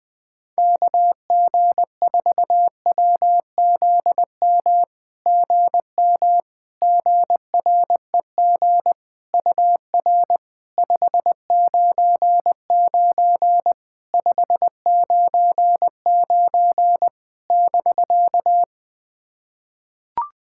Morse code, heard as a keyed tone, about 20 wpm, 700 Hz. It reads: KG4WZM GM GREG UR 599 599 <BK>
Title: SOTA QSO Practice 20wpm